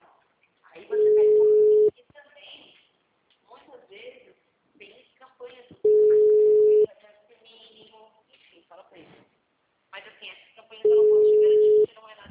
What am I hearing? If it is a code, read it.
Ringback tone (Europe, 425 Hz), with speech.
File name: out- -2010- - - .457